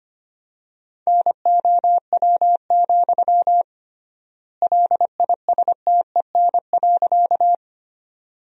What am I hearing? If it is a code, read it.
Morse code, heard as a keyed tone, about 25 wpm, 700 Hz. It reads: NOW, LISTEN.
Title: 3x Word-Spacing - Sentences from Top 400 Words - 25wpm